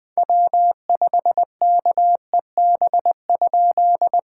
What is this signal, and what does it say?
Morse code, heard as a keyed tone, about 20 wpm, 690 Hz. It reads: W5KEB?
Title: POTA QSO Practice - 20wpm